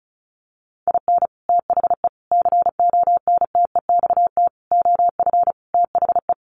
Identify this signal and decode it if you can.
Morse code, heard as a keyed tone, about 34 wpm, 700 Hz. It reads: IN THE CONTEXT OF THE
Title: Sets of 5 Words 35wpm